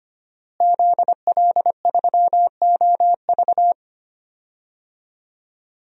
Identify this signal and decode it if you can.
Morse code, heard as a keyed tone, about 25 wpm, 700 Hz. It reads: ZL3OV
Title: DXCC-100 Call Signs - 5 Characters - Repeated 3x - 25wpm